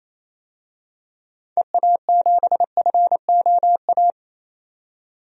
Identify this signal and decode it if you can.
Morse code, heard as a keyed tone, about 28 wpm, 700 Hz. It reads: EA7FOA
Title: DXCC-100 Call Signs - 6 Characters - 28wpm